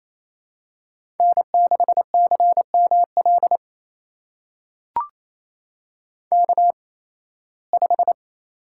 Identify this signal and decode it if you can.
Morse code, heard as a keyed tone, about 28 wpm, spaced as for 18 wpm, 700 Hz. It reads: N6CML K5
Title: USA Call Signs - 8x Character Spacing - 28wpm